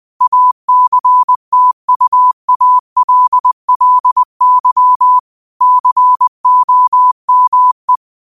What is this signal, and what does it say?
Morse code, heard as a keyed tone, about 20 wpm, 1000 Hz. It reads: ACTUALLY COME